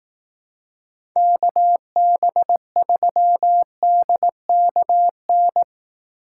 Morse code, heard as a keyed tone, about 18 wpm, 700 Hz. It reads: KB3DKN
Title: Speed Racing - USA 2x3 Call Signs 15wpm